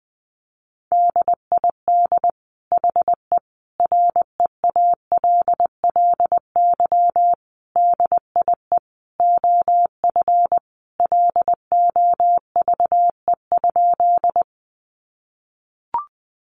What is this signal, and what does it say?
Morse code, heard as a keyed tone, about 20 wpm, 700 Hz. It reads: DID HE REALLY DIE OF LOVE?
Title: Sentences from Top 1300 Words 20wpm